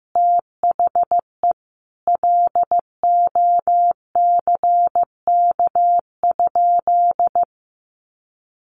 Morse code, heard as a keyed tone, about 15 wpm, 700 Hz. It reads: THE LOCK?
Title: Sentences from Top 1300 Words 15wpm